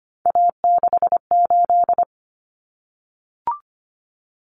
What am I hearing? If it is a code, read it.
Morse code, heard as a keyed tone, about 25 wpm, 700 Hz. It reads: A68
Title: Common Cut Numbers - 3-Digit 25wpm